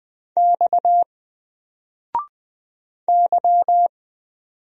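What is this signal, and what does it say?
Morse code, heard as a keyed tone, about 20 wpm, 700 Hz. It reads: X Y
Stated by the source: Single letter and number Morse drill at 20 wpm, from N4LCD.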